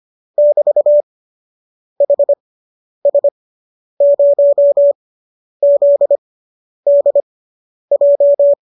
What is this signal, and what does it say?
Morse code, heard as a keyed tone, about 25 wpm, spaced as for 10 wpm, 590 Hz. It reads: <BT>HS0ZDJ